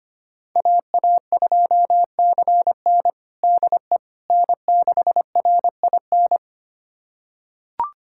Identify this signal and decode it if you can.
Morse code, heard as a keyed tone, about 25 wpm, 700 Hz. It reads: AA2CN DE N6RIN
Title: USA Callsign DE Callsign - 5 Characters - 25wpm